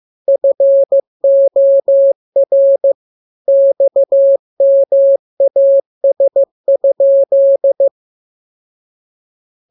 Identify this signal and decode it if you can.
Morse code, heard as a keyed tone, about 15 wpm, 550 Hz. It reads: FOR XMAS?